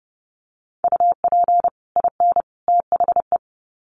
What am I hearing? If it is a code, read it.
Morse code, heard as a keyed tone, about 30 wpm, 700 Hz. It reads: UP IN THE
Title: Sets of 3 Words 30wpm